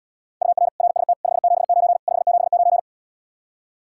Morse code, heard as a keyed tone, about 78 wpm, 700 Hz. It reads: UR RST 579 579